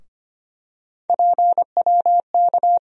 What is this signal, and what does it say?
Morse code, heard as a keyed tone, about 25 wpm, 700 Hz. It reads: PWK
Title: USA Call Sign Postfixes - 3 Letters 25wpm